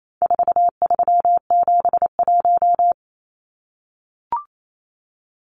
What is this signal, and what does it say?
Morse code, heard as a keyed tone, about 28 wpm, 700 Hz. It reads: 4371
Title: Numbers - 4-Digit 28wpm